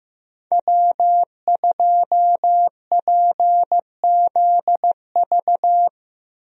Morse code, heard as a keyed tone, about 15 wpm, 700 Hz. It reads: W2PZV
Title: USA Call Signs - 5 Characters - 15wpm